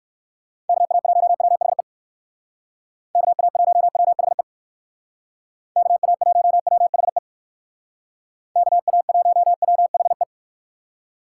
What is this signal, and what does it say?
Morse code, heard as a keyed tone, about 58 wpm, 700 Hz. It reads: KA1WHE KA1WHE KA1WHE KA1WHE